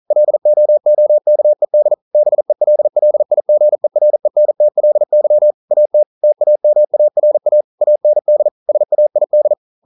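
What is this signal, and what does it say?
Morse code, heard as a keyed tone, about 42 wpm, 600 Hz. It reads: LOOKED BELLIGERENTLY AT TAMARA AND SAID
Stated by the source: Morse code practice at 41 wpm, from CW med Gnister.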